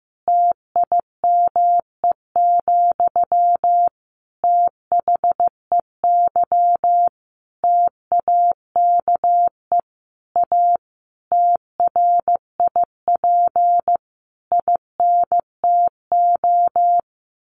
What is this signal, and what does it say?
Morse code, heard as a keyed tone, about 15 wpm, 700 Hz. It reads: TIME, THEY TAKE A TRIP INTO
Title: Sentences from Top 1300 Words 15wpm